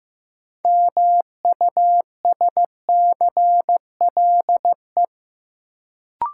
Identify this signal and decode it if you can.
Morse code, heard as a keyed tone, about 15 wpm, 700 Hz. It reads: MUSCLE
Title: Sentences from Top 1300 Words - New Word Drill - 15wpm